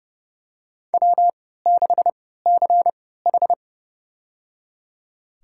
Morse code, heard as a keyed tone, about 30 wpm, spaced as for 17 wpm, 700 Hz. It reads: W6CH
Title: USA Call Signs - 4 Characters - 3x Character Spacing - 30wpm